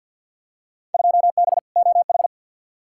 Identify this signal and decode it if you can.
Morse code, heard as a keyed tone, about 49 wpm, 700 Hz. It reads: 2B OH
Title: Field Day Exchanges - 50wpm